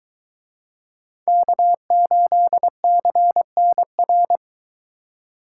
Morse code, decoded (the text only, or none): K8CNR